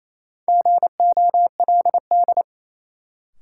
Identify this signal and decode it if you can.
Morse code, heard as a keyed tone, about 28 wpm, 700 Hz. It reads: GOLD
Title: Sentences from Top 1000 Words - Review Part 3 - New Word Drill - 28wpm